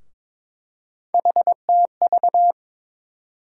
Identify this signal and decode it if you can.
Morse code, heard as a keyed tone, about 22 wpm, 700 Hz. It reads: HTV
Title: USA Call Sign Postfixes - 3 Letters 22wpm